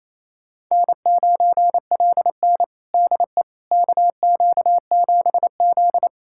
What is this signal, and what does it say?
Morse code, heard as a keyed tone, about 28 wpm, 700 Hz. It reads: N9LN DE KQ7Z
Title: USA Callsign DE Callsign - 4 Characters - 28wpm